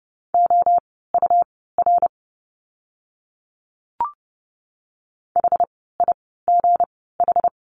Morse code, heard as a keyed tone, about 30 wpm, spaced as for 17 wpm, 700 Hz. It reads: OUR HIGH
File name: course / 30:10wpm / 237 Lesson 38f - Letter q - Words 30wpm